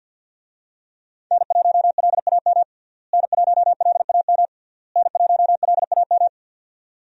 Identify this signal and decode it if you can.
Morse code, heard as a keyed tone, about 46 wpm, 700 Hz. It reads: N1LAM N1LAM N1LAM
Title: USA Call Signs - 5 Characters - Repeated 3x - 50wpm